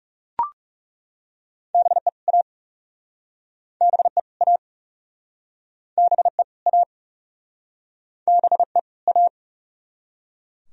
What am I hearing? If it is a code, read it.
Morse code, heard as a keyed tone, about 40 wpm, 700 Hz. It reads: BE A BE A BE A BE A